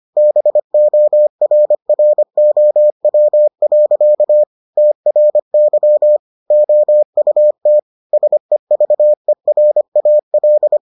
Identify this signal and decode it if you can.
Morse code, heard as a keyed tone, about 25 wpm, 600 Hz. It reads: BORROW. TRY OUT SEVERAL